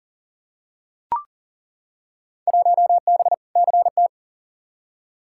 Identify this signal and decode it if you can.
Morse code, heard as a keyed tone, about 40 wpm, 700 Hz. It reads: 1B CT
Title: Field Day Exchanges Encore - 40wpm